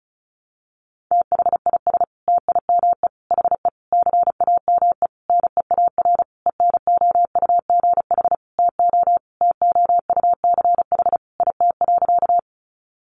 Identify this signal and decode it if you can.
Morse code, heard as a keyed tone, about 34 wpm, 700 Hz. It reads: THIS TIME HE CAME NEAR ENOUGH TO TOUCH IT.
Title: Sentences from Top 1200 Words 35wpm